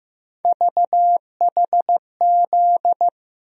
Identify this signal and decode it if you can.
Morse code, heard as a keyed tone, about 15 wpm, 700 Hz. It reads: VHZ
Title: USA Call Sign Postfixes - 3 Letters 15wpm